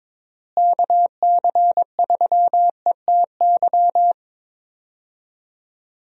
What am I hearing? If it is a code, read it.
Morse code, heard as a keyed tone, about 21 wpm, 700 Hz. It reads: KC3ETY
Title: USA Call Signs - 6 Characters - Repeated 2x - 22wpm